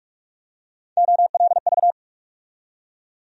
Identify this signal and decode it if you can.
Morse code, heard as a keyed tone, about 45 wpm, 700 Hz. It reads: OLV
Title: USA Call Sign Postfixes - 3 Letters 45wpm